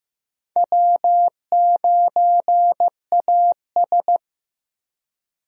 Morse code, heard as a keyed tone, about 15 wpm, 700 Hz. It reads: W9AS
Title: USA Call Signs - 4 Characters - Repeated 3x - 15wpm